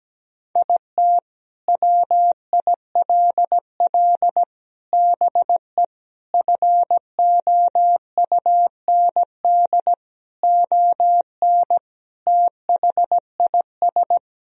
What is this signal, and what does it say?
Morse code, heard as a keyed tone, about 17 wpm, 700 Hz. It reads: IT WILL BE FOUND ON THIS
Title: Sentences from Top 1000 Words - Review Part 1 - 17wpm